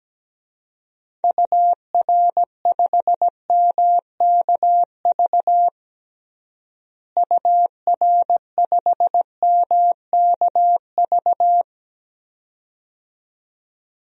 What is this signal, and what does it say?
Morse code, heard as a keyed tone, about 17 wpm, 700 Hz. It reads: UR5MKV UR5MKV
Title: DXCC-100 Call Signs - 6 Characters - Repeated 2x - 17wpm